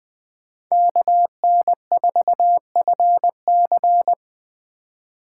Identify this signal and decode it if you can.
Morse code, heard as a keyed tone, about 21 wpm, 700 Hz. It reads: KN4FC
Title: USA Call Signs - 2x Character Spacing - 20wpm